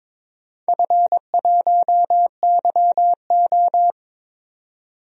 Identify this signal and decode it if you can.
Morse code, heard as a keyed tone, about 23 wpm, 700 Hz. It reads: F1YO
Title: DXCC-100 Call Signs - 4 Characters - 22wpm